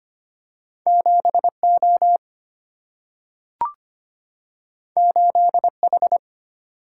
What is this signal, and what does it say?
Morse code, heard as a keyed tone, about 25 wpm, 700 Hz. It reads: 7 O 8 H
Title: Single-Letter Single-Number 25wpm